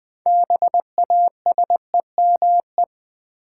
Morse code, heard as a keyed tone, about 20 wpm, 700 Hz. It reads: BASEME
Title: Build A Word - 2 Letters At A Time - 8-Letter Words - 20wpm